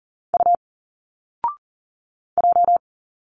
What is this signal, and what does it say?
Morse code, heard as a keyed tone, about 40 wpm, 700 Hz. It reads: U J
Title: Single Letters 40wpm